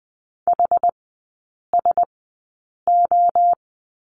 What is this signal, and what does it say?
Morse code, heard as a keyed tone, about 20 wpm, spaced as for 8 wpm, 700 Hz. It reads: HSO
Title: 060 Lesson 11e - Letter h - Three Characters 20wpm